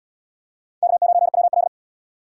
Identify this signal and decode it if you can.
Morse code, heard as a keyed tone, about 78 wpm, 700 Hz. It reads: K9WL